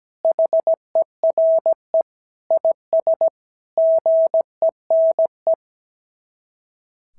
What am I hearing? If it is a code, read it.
Morse code, heard as a keyed tone, about 16 wpm, 650 Hz. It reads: HERE IS GENE
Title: SOTA QSO Practice 17wpm